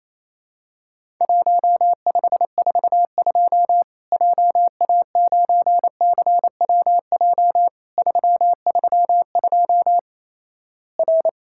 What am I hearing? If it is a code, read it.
Morse code, heard as a keyed tone, about 28 wpm, 700 Hz. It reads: 1542 JA9CWJ 332 R
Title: Worked All Europe DX Contest - QTC Exchange - 28wpm